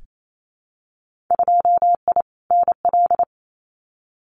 Morse code, heard as a keyed tone, about 28 wpm, 700 Hz. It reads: 2I NL
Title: Winter Field Day Exchanges Encore 28wpm